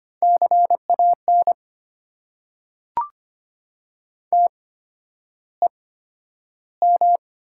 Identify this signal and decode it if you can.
Morse code, heard as a keyed tone, about 25 wpm, spaced as for 11 wpm, 700 Hz. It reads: CAN TEM
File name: Top 90 QSO Elements - 8x Character Spacing - 25wpm